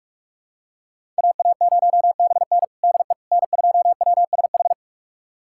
Morse code, heard as a keyed tone, about 46 wpm, 700 Hz. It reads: AA0BN DE N2WSH
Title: USA Callsign DE Callsign - 5 Characters - 45wpm